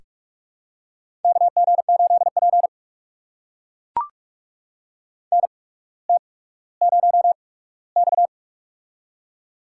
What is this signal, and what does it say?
Morse code, heard as a keyed tone, about 44 wpm, spaced as for 19 wpm, 700 Hz. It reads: KG8P NT0X
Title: USA Call Signs - 4 Characters - 8x Character Spacing - 45wpm